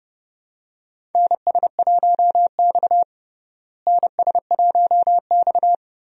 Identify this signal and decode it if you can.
Morse code, heard as a keyed tone, about 32 wpm, 700 Hz. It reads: NS1X NS1X
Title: USA Call Signs - 4 Characters - Repeated 2x - 30wpm